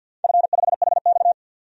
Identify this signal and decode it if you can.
Morse code, heard as a keyed tone, about 51 wpm, 700 Hz. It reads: F5HX
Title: DXCC-100 Call Signs - 4 Characters - 50wpm